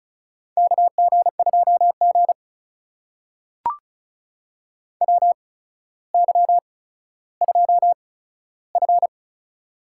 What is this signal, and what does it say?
Morse code, heard as a keyed tone, about 36 wpm, spaced as for 15 wpm, 700 Hz. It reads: KG2G WY2F
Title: USA Call Signs - 4 Characters - 8x Character Spacing - 35wpm